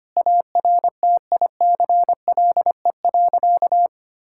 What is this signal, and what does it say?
Morse code, heard as a keyed tone, about 25 wpm, 700 Hz. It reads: ARTICLE.
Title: Sentences from Top 1000 Words - Review Part 5 - 25wpm